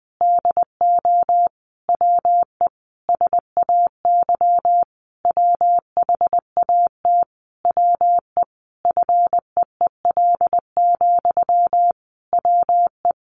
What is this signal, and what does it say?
Morse code, heard as a keyed tone, about 20 wpm, 700 Hz. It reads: DO WE SAY WHAT WE FEEL, WE
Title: Sentences from Top 1200 Words 20wpm